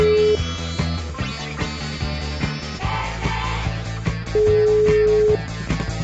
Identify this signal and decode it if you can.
Ringback tone (Europe, 425 Hz), with noise.